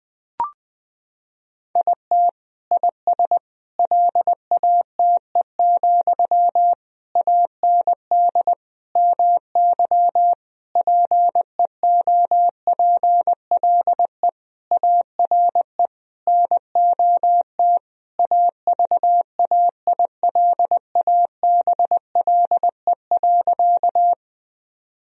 Morse code, heard as a keyed tone, about 21 wpm, 700 Hz. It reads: IT IS LATE, AND MY PEOPLE ARE NOT AVAILABLE.
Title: Sentences from Top 1000 Words - Review Part 3 - 20wpm